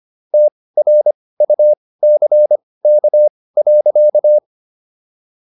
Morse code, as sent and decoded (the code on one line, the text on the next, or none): - .-. ..- -.-. -.- .-.-.-
TRUCK.